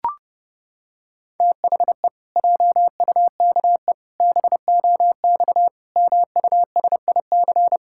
Morse code, heard as a keyed tone, about 32 wpm, 700 Hz. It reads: THE JUKE BOX MUSIC
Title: Pangram Challenge v2 - 30wpm